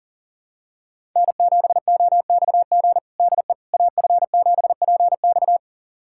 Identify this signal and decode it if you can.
Morse code, heard as a keyed tone, about 37 wpm, 700 Hz. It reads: N7OXG DE AF7PX
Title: USA Callsign DE Callsign - 5 Characters - 40wpm